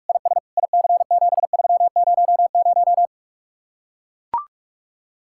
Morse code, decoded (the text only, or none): IS IC7300